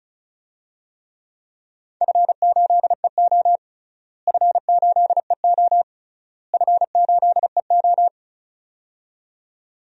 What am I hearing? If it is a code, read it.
Morse code, heard as a keyed tone, about 36 wpm, 700 Hz. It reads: F8EO F8EO F8EO
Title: DXCC-100 Call Signs - 4 Characters - Repeated 3x - 35wpm